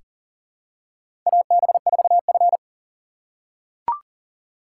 Morse code, heard as a keyed tone, about 38 wpm, 700 Hz. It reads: AB4F
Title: USA Call Signs - 4 Characters - 4x Character Spacing - 40wpm